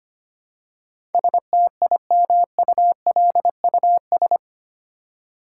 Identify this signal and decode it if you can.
Morse code, heard as a keyed tone, about 25 wpm, 700 Hz. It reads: STIMULUS
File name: Eight Letter Words - 25wpm